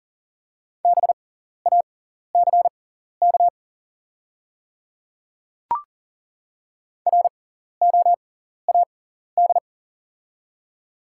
Morse code, heard as a keyed tone, about 39 wpm, spaced as for 14 wpm, 690 Hz. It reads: BACK ROAD
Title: 6x Character Spacing - Top 500 Words - 40wpm